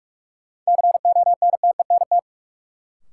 Morse code, heard as a keyed tone, about 47 wpm, 700 Hz. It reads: CONTENT
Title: Sentences from Top 900 Words - New Word Drill - 45wpm